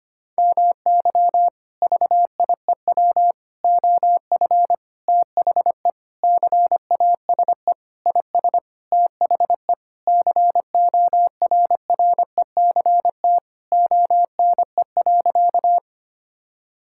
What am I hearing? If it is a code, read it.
Morse code, heard as a keyed tone, about 25 wpm, 700 Hz. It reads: MY VIEW OF THE CASE IS THE CORRECT ONE.
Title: Sentences from Top 1000 Words - Review Part 6 - 25wpm